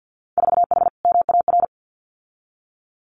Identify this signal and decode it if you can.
Morse code, heard as a keyed tone, about 50 wpm, 700 Hz. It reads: 4H MAR
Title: Winter Field Day Exchanges Encore 50wpm